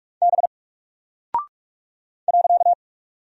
Morse code, heard as a keyed tone, about 44 wpm, 700 Hz. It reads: B .
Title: Single Character (Letter:Number:Punctuation:Prosign) 45wpm